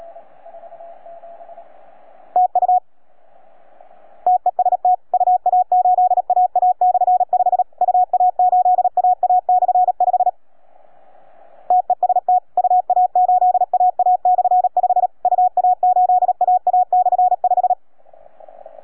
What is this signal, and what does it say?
Morse code, heard as a keyed tone, about 37 wpm, 730 Hz. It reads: TU TEST UA8AA/5 UA8AA/5 TEST UA8AA/5 UA8AA/5